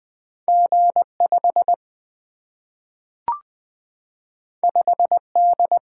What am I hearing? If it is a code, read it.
Morse code, heard as a keyed tone, about 20 wpm, 700 Hz. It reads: G 5 5 D
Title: Single-Letter Single-Number 20wpm